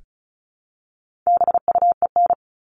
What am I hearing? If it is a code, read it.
Morse code, heard as a keyed tone, about 34 wpm, 700 Hz. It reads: BUEN
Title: All Cut Numbers - 4-Digit 35wpm